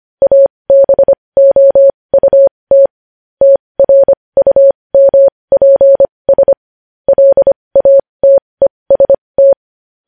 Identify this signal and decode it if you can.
Morse code, heard as a keyed tone, about 25 wpm, spaced as for 20 wpm, 550 Hz. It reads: ABOUT TRUMPS LATEST
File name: podcast / News headlines-1x-25x20